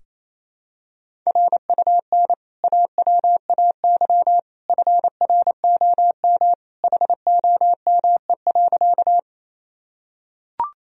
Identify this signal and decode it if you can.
Morse code, heard as a keyed tone, about 28 wpm, 700 Hz. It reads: RUN AWAY FROM HOME.
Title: Sentences from Top 1000 Words - Review Part 1 - 28wpm